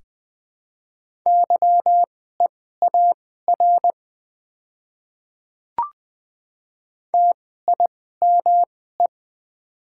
Morse code, heard as a keyed tone, about 20 wpm, spaced as for 14 wpm, 700 Hz. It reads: YEAR TIME
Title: 2x Character Spacing - Top 300 Words - 20wpm